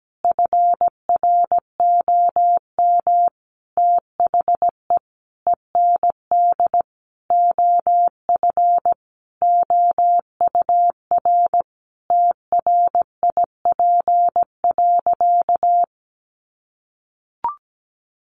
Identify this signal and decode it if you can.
Morse code, heard as a keyed tone, about 17 wpm, 700 Hz. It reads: FROM THE END OF OUR TRIP.
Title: Sentences from Top 1300 Words 17wpm